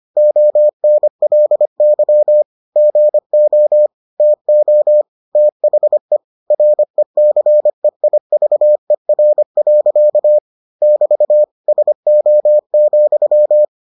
Morse code, heard as a keyed tone, about 25 wpm, 600 Hz. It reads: ONLY GO TO THE RECEIVER. <BT> SO,